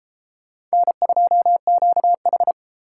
Morse code, heard as a keyed tone, about 30 wpm, 700 Hz. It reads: N2QH